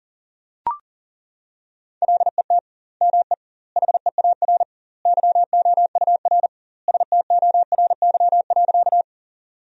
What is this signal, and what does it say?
Morse code, heard as a keyed tone, about 40 wpm, 700 Hz. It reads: LET ME HEAR YOUR STORY.
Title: 2x Word-Spacing - Sentences from Top 300 Words - 40wpm